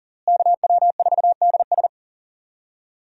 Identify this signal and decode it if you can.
Morse code, heard as a keyed tone, about 40 wpm, 700 Hz. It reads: KW4DS
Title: USA Call Signs - 5 Characters - Repeated 2x - 40wpm